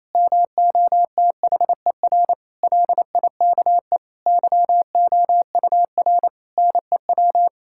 Morse code, heard as a keyed tone, about 27 wpm, 700 Hz. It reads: MOTHER LIKE YOUR NEW
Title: Sentences from Top 1000 Words - Review Part 5 - 28wpm